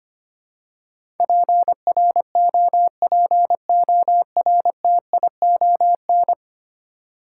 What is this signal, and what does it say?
Morse code, heard as a keyed tone, about 25 wpm, 700 Hz. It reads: PROPORTION
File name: 10 Letter Words - First 3 Letters Given - 25wpm